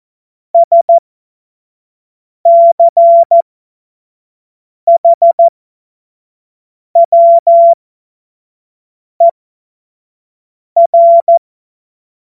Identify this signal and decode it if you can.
Morse code, heard as a keyed tone, about 14 wpm, spaced as for 5 wpm, 680 Hz. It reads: SCHWER
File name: Morsekurs / Witze / Witz 13 Mai 2025 25 F70